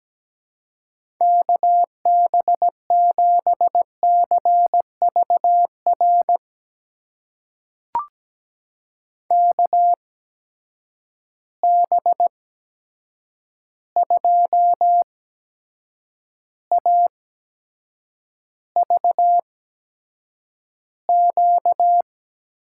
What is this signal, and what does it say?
Morse code, heard as a keyed tone, about 17 wpm, spaced as for 7 wpm, 700 Hz. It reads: KB7CVR KB2AVQ
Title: USA Call Signs - 6 Characters - 8x Character Spacing - 17wpm